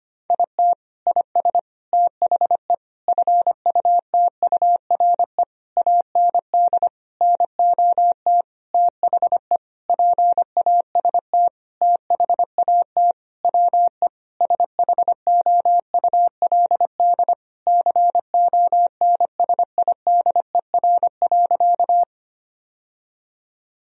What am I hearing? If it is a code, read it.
Morse code, heard as a keyed tone, about 25 wpm, 700 Hz. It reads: IT IS THE FUTURE AND NOT THE PAST THAT WE SHOULD CONSIDER.
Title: Sentences from Top 1000 Words - Review Part 4 - 25wpm